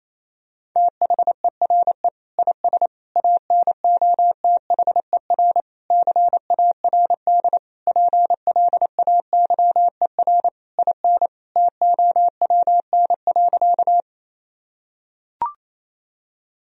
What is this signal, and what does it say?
Morse code, heard as a keyed tone, about 28 wpm, 700 Hz. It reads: THERE IS ANOTHER CARD PLAYER IN TOWN.
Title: Sentences from Top 1000 Words - Review Part 4 - 28wpm